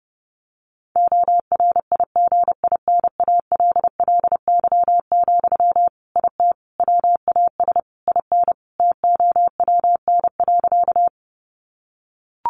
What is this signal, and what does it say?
Morse code, heard as a keyed tone, about 30 wpm, 700 Hz. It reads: ORIGINALLY, IT WAS IN TOWN.
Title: Sentences from Top 1500 Words 30wpm